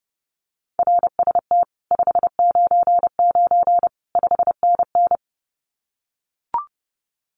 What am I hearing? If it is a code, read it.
Morse code, heard as a keyed tone, about 30 wpm, 700 Hz. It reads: RST 599 5NN
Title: Common QSO Phrases Encore 30wpm